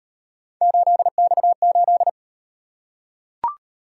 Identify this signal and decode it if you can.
Morse code, heard as a keyed tone, about 38 wpm, 700 Hz. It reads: 8X8